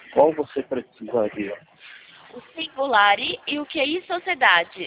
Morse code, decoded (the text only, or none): none